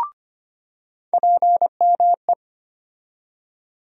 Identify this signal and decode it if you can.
Morse code, heard as a keyed tone, about 25 wpm, 700 Hz. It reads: PME
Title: USA Call Sign Postfixes - 3 Letters 25wpm